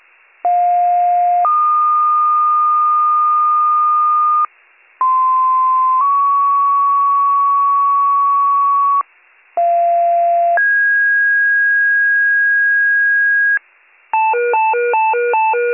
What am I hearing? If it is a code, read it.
Two-tone page, heard as a one-second tone, then a three-second tone, about 710 then 1220 Hz, about 1040 then 1120 Hz, about 690 then 1650 Hz. It is heard from 0.4 to 13.6 s.